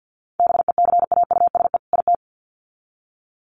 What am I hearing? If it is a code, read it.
Morse code, heard as a keyed tone, about 50 wpm, 700 Hz. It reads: BECAUSE IT
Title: Sets of 2 Words 50wpm